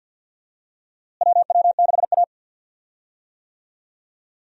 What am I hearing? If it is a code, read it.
Morse code, heard as a keyed tone, about 52 wpm, 700 Hz. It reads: WW6A
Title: USA Call Signs - 4 Characters - Repeated 2x - 50wpm